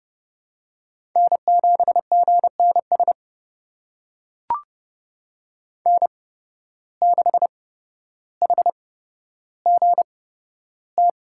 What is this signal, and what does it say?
Morse code, heard as a keyed tone, about 29 wpm, spaced as for 13 wpm, 700 Hz. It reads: N7GNS N6HGT
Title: USA Call Signs - 5 Characters - 8x Character Spacing - 30wpm